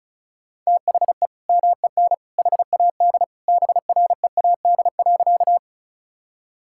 Morse code, heard as a keyed tone, about 34 wpm, 700 Hz. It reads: THE MEN HAD BREAD.
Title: Sentences from Top 1600 Words 35wpm